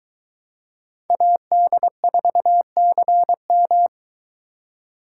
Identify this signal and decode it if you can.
Morse code, heard as a keyed tone, about 23 wpm, 700 Hz. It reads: AD4CM